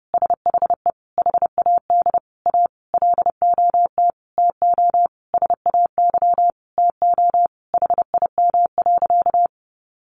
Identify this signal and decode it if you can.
Morse code, heard as a keyed tone, about 30 wpm, 700 Hz. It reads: SHE HAD A LOT TO SAY TO HIM.
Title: Sentences from Top 1000 Words 30wpm